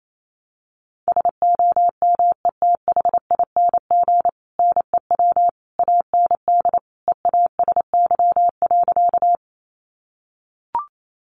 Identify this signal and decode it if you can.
Morse code, heard as a keyed tone, about 28 wpm, 700 Hz. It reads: SOMETHING NEW AND EASY.
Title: Sentences from Top 700 Words 28wpm